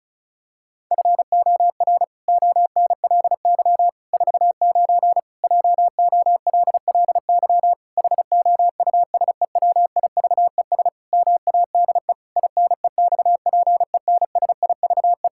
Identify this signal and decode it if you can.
Morse code, heard as a keyed tone, about 36 wpm, 700 Hz. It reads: FOR ONLY 49 JOLLY HOUSEWIVES MADE INEXPENSIVE
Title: Pangram Challenge v2 - 35wpm